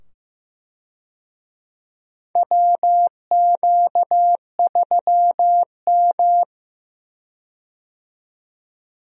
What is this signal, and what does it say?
Morse code, heard as a keyed tone, about 15 wpm, 700 Hz. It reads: WQ3M
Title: USA Call Signs - 4 Characters - 15wpm